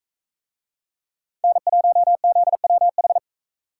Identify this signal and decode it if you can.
Morse code, heard as a keyed tone, about 41 wpm, 700 Hz. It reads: N1ZWH